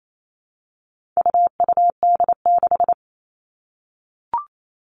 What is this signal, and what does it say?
Morse code, heard as a keyed tone, about 28 wpm, 700 Hz. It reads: UUD6
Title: All Cut Numbers - 4-Digit 28wpm